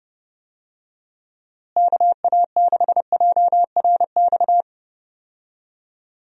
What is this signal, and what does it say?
Morse code, heard as a keyed tone, about 31 wpm, 700 Hz. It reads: KA6JRX
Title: USA Call Signs - 6 Characters - Repeated 3x - 30wpm